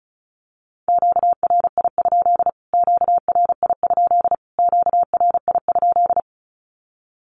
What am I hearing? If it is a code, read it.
Morse code, heard as a keyed tone, about 34 wpm, 700 Hz. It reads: QRI? QRI? QRI?
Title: Q-Codes Repeated 3x 35wpm